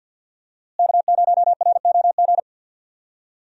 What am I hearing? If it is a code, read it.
Morse code, heard as a keyed tone, about 52 wpm, 700 Hz. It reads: K0ROG
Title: USA Call Signs - 5 Characters - Repeated 3x - 50wpm